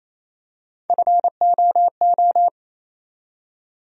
Morse code, heard as a keyed tone, about 28 wpm, 700 Hz. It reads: FOO
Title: USA Call Sign Postfixes - 3 Letters 28wpm